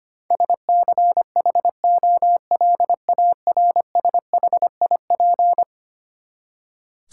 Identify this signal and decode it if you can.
Morse code, heard as a keyed tone, about 25 wpm, 700 Hz. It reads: SCHOLARSHIP
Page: morse-code-ninja-practice-25wpm